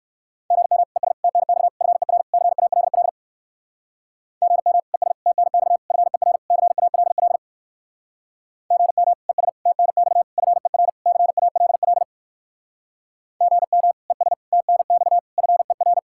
Morse code, heard as a keyed tone, about 58 wpm, 700 Hz. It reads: GM ES TNX FER CALL GM ES TNX FER CALL GM ES TNX FER CALL GM ES TNX FER